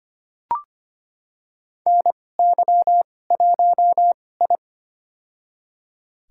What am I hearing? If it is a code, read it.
Morse code, heard as a keyed tone, about 25 wpm, spaced as for 18 wpm, 700 Hz. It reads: NY1I
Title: USA Call Signs - 4 Characters - 2x Character Spacing - 25wpm